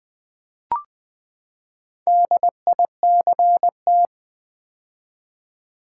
Morse code, heard as a keyed tone, about 20 wpm, 700 Hz. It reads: DICT
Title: Build A Word - 4 Letters At A Time - 10-Letter Words - 20wpm